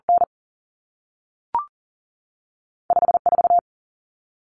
Morse code, heard as a keyed tone, about 40 wpm, 700 Hz. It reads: N 5 4
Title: Common Cut Numbers - 2-Digit 40wpm